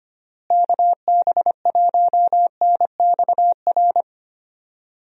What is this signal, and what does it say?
Morse code, heard as a keyed tone, about 25 wpm, 700 Hz. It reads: KB1NXR
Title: USA Call Signs - 6 Characters - Repeated 3x - 25wpm